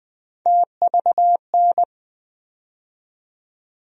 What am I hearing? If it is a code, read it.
Morse code, heard as a keyed tone, about 20 wpm, 700 Hz. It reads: TVN
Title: USA Call Sign Postfixes - 3 Letters 20wpm